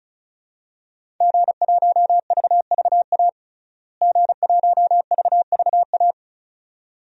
Morse code, heard as a keyed tone, about 34 wpm, 700 Hz. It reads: G1VVA G1VVA
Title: DXCC-100 Call Signs - 5 Characters - Repeated 2x - 35wpm